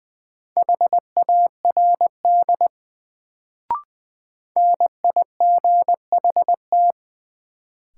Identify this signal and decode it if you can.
Morse code, heard as a keyed tone, about 20 wpm, 700 Hz. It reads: HARD NIGHT
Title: Sentences from Top 300 Words - New Word Drill - 20wpm